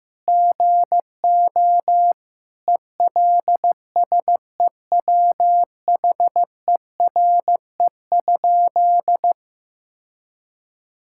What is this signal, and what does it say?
Morse code, heard as a keyed tone, about 15 wpm, 700 Hz. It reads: GO ELSEWHERE?
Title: Sentences from Top 1600 Words 15wpm